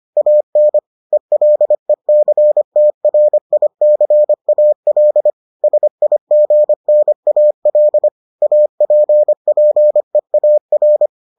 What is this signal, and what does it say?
Morse code, heard as a keyed tone, about 25 wpm, 600 Hz. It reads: AN ELECTRICAL SIGNAL APPEAR